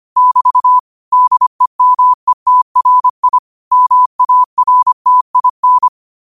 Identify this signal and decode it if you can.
Morse code, heard as a keyed tone, about 25 wpm, 1000 Hz. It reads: <BT> DEMETRI MARTIN
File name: qotdfiles / 25wpm